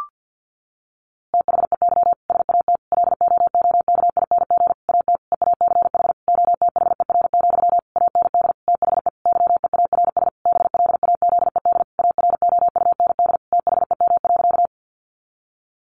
Morse code, heard as a keyed tone, about 50 wpm, 700 Hz. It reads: THEY SAT LOOKING AT EACH OTHER, AND THE YEARS BLAZED AROUND THEM.
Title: A Sound Of Thunder 50wpm